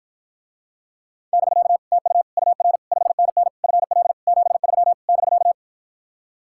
Morse code, heard as a keyed tone, about 54 wpm, 700 Hz. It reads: <BK> TU UR 5NN FL 73 <BK>